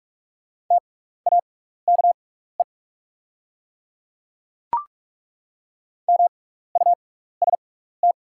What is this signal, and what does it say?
Morse code, heard as a keyed tone, about 43 wpm, spaced as for 15 wpm, 690 Hz. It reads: TAKE MUST
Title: 6x Character Spacing - Top 200 Words - 45wpm